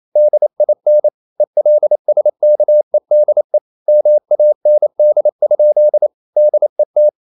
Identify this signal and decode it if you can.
Morse code, heard as a keyed tone, about 28 wpm, 600 Hz. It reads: DIN ELSKEDE MAND? DET
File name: Ferie QRQ 27wpm